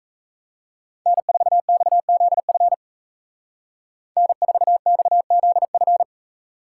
Morse code, heard as a keyed tone, about 36 wpm, 700 Hz. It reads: N4XZF N4XZF